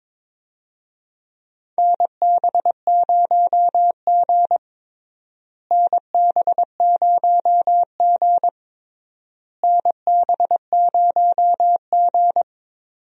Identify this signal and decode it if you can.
Morse code, heard as a keyed tone, about 22 wpm, 700 Hz. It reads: NB0G NB0G NB0G
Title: USA Call Signs - 4 Characters - Repeated 3x - 22wpm